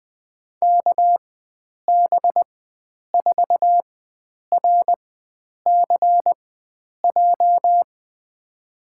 Morse code, heard as a keyed tone, about 20 wpm, spaced as for 9 wpm, 700 Hz. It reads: KB4RCJ
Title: USA Call Signs - 6 Characters - 4x Character Spacing - 20wpm